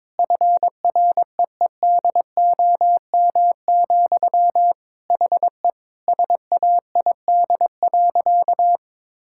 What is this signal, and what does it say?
Morse code, heard as a keyed tone, about 22 wpm, 700 Hz. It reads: FREEDOM, HE SAID.
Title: Sentences from Top 1000 Words - Review Part 5 - 22wpm